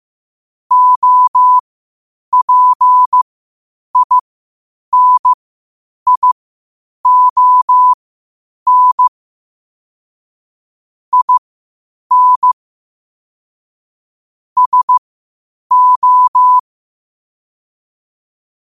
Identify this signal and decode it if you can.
Morse code, heard as a keyed tone, about 15 wpm, spaced as for 8 wpm, 1000 Hz. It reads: OPINION IN SO
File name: qotdfiles / 8wpm